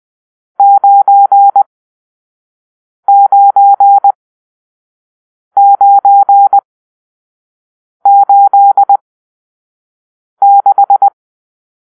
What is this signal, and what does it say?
Morse code, heard as a keyed tone, about 20 wpm, spaced as for 5 wpm, 800 Hz. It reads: 99986